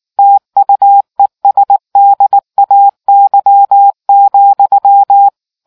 Morse code, heard as a keyed tone, about 19 wpm, 800 Hz. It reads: TUESDAY,